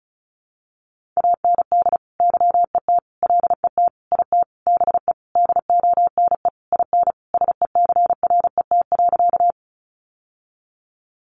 Morse code, heard as a keyed tone, about 34 wpm, 700 Hz. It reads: AND YET LET IT BE DONE IN SECRET.
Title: Sentences from Top 1200 Words 35wpm